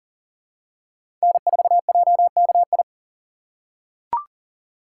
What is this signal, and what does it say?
Morse code, heard as a keyed tone, about 40 wpm, 700 Hz. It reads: N4JKI
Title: USA Call Signs - 6x Character Spacing - 40wpm